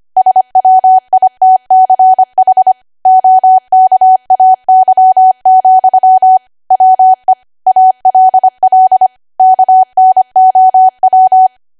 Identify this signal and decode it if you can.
Morse code, heard as a keyed tone, about 25 wpm, 750 Hz. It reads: SWITCH OKAY, WE ALL KNOW
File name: Archive / 25 WPM / 25WPM